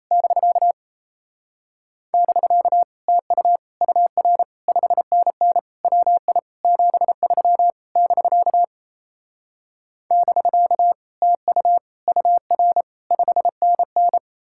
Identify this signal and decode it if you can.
Morse code, heard as a keyed tone, about 37 wpm, 700 Hz. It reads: <BK> <BK> TU UR 5NN WI 73 <BK> <BK> TU UR 5NN